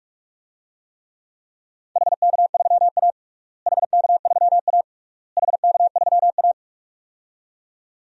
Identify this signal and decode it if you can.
Morse code, heard as a keyed tone, about 46 wpm, 700 Hz. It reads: HK3A HK3A HK3A
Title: DXCC-100 Call Signs - 4 Characters - Repeated 3x - 45wpm